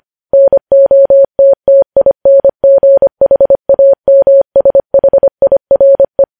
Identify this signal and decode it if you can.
Morse code, heard as a keyed tone, about 25 wpm, 550 Hz. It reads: NOTTINGHAMSHIRE
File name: podcast / News headlines-1x-25x25